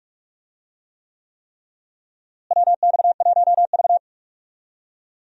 Morse code, heard as a keyed tone, about 42 wpm, 700 Hz. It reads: WX1V
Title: USA Call Signs - 4 Characters - Repeated 3x - 45wpm